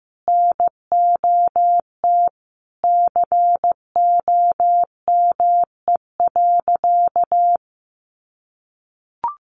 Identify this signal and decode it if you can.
Morse code, heard as a keyed tone, about 15 wpm, 700 Hz. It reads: NOT COME.
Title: Sentences from Top 1500 Words 15wpm